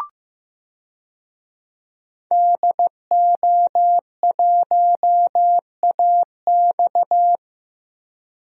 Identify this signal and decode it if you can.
Morse code, heard as a keyed tone, about 15 wpm, 700 Hz. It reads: DO1AX
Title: DXCC-100 Call Signs - 5 Characters - Repeated 3x - 15wpm